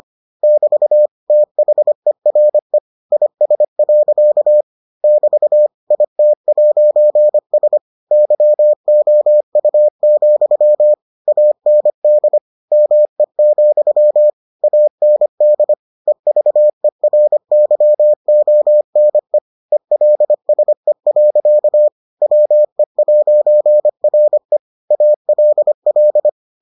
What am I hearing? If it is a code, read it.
Morse code, heard as a keyed tone, about 25 wpm, 600 Hz. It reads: <BT> THERE IS. <BT> IT'S YOU, AND ME, AND EVERYONE ELSE. WE'RE ALL